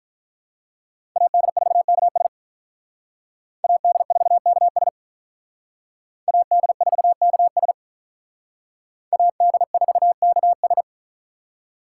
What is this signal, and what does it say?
Morse code, heard as a keyed tone, about 43 wpm, 700 Hz. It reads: AD4KS AD4KS AD4KS AD4KS
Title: Speed Racing - USA 2x2 Call Signs 35wpm